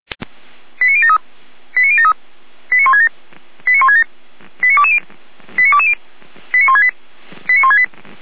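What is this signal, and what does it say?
Five-tone selective call (ZVEI), heard as quick runs of five tones, tones of about 70 ms. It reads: 89073 89073 89167 89167 89220 89220 89168 89168
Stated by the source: ZVEI five-tone call received by SDR, from the audio server of the Feuerwehr Orlamünde (fire brigade).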